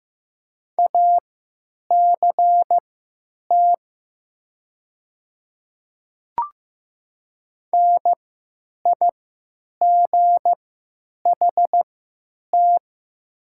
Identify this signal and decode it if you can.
Morse code, heard as a keyed tone, about 15 wpm, spaced as for 9 wpm, 700 Hz. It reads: ACT NIGHT
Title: 3x Character Spacing - Top 300 Words - 15wpm